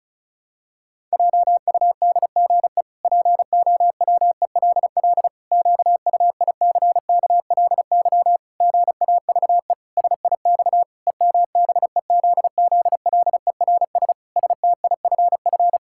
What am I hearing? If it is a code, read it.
Morse code, heard as a keyed tone, about 34 wpm, 700 Hz. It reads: JUDGE POWELL QUICKLY GAVE SIX EMBEZZLERS STIFF
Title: Pangram Challenge v2 - 35wpm